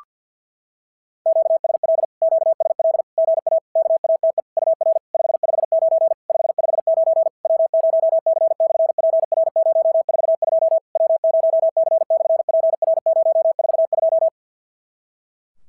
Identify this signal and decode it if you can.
Morse code, heard as a keyed tone, about 51 wpm, 650 Hz. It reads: QSL QSL GA KATE UR 559 559 W0C/PR042 W0C/PR042
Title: SOTA QSO Practice 50wpm